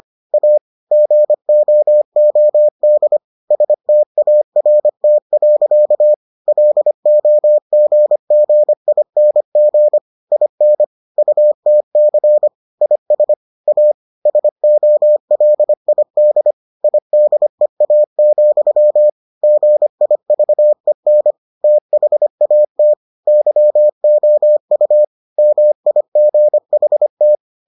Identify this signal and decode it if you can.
Morse code, heard as a keyed tone, about 25 wpm, 600 Hz. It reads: A GOOD START. LOGGING IN UTC IS A SOLID IDEA, GIVEN THAT YOU MIGHT